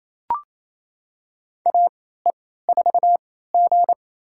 Morse code, heard as a keyed tone, about 28 wpm, spaced as for 16 wpm, 700 Hz. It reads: AE4G
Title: USA Call Signs - 4 Characters - 3x Character Spacing - 28wpm